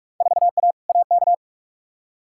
Morse code, heard as a keyed tone, about 43 wpm, 700 Hz. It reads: 4A AK
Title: Field Day Exchanges - 45wpm